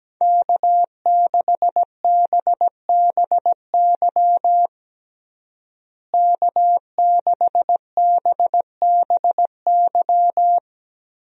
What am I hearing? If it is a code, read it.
Morse code, heard as a keyed tone, about 17 wpm, 700 Hz. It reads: K6BBY K6BBY
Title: USA Call Signs - 5 Characters - Repeated 2x - 17wpm